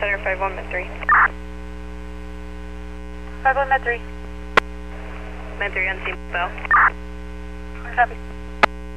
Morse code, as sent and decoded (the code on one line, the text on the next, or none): none